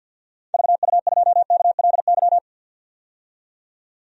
Morse code, heard as a keyed tone, about 53 wpm, 700 Hz. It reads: VU2KLY
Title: DXCC-100 Call Signs - 6 Characters - 50wpm